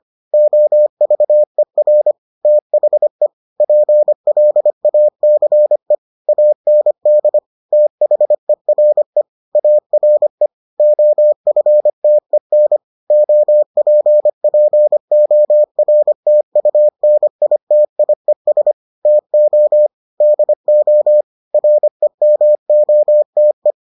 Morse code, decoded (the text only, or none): OVER THE PLACE AND THERE ARE OFTEN OPPORTUNITIES TO DO REMOTE